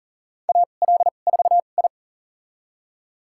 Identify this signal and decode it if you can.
Morse code, heard as a keyed tone, about 40 wpm, spaced as for 29 wpm, 700 Hz. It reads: AL4I
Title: USA Call Signs - 4 Characters - 2x Character Spacing - 40wpm